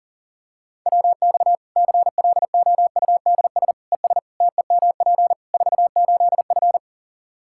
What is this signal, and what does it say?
Morse code, heard as a keyed tone, about 40 wpm, 700 Hz. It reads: WX CLOUDS ES TEMP 48F
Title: Practice QSO - Exchange 2 - Info - 40wpm